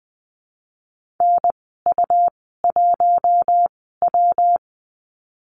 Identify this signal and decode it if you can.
Morse code, heard as a keyed tone, about 20 wpm, spaced as for 14 wpm, 700 Hz. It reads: NU1W
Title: 128 Lesson 21g - Number 6 - Call Signs 20wpm